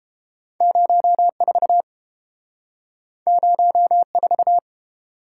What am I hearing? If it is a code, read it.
Morse code, heard as a keyed tone, about 30 wpm, 700 Hz. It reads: 04 04